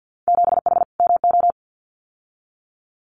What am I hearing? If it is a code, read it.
Morse code, heard as a keyed tone, about 50 wpm, 700 Hz. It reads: 7H MO
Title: Winter Field Day Exchanges Encore 50wpm